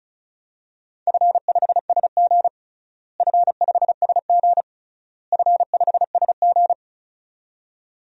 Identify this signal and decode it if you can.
Morse code, heard as a keyed tone, about 34 wpm, 700 Hz. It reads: F5SG F5SG F5SG
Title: DXCC-100 Call Signs - 4 Characters - Repeated 3x - 35wpm